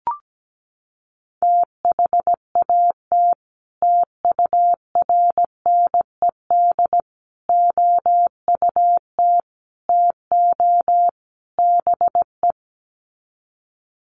Morse code, heard as a keyed tone, about 17 wpm, 700 Hz. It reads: THAT TURNED OUT TO BE
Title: Sets of 5 Words Encore2 17wpm